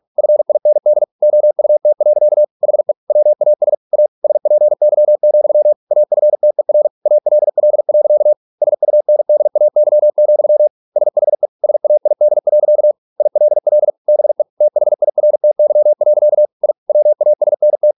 Morse code, heard as a keyed tone, about 46 wpm, 600 Hz. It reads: FIND OUT. HE WAS A SPY, AFTER ALL. SUNDAY, SHE SAID. ILL BE THIRTY. I WASNT